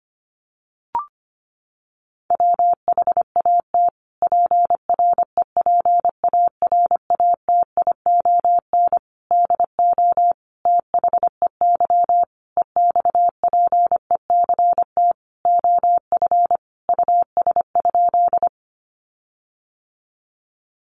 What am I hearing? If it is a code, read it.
Morse code, heard as a keyed tone, about 25 wpm, 700 Hz. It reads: WHAT PREPARATION DO THEY EXPECT OF US?
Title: Sentences from Top 1400 Words 25wpm